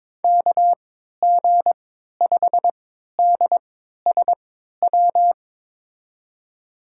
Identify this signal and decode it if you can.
Morse code, heard as a keyed tone, about 22 wpm, spaced as for 13 wpm, 700 Hz. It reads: KG5DSW
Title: USA Call Signs - 6 Characters - 3x Character Spacing - 22wpm